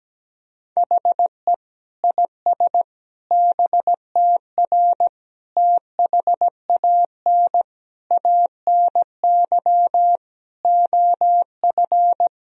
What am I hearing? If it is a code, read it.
Morse code, heard as a keyed tone, about 17 wpm, 700 Hz. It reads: HE IS BTR THAN ANY OF
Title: Sentences from Top 400 Words and Abbreviations 17wpm